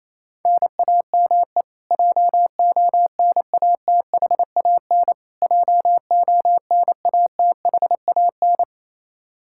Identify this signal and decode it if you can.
Morse code, heard as a keyed tone, about 28 wpm, 700 Hz. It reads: NAME JONATHAN JONATHAN
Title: Practice QSO - Exchange 1 - Beginning - 28wpm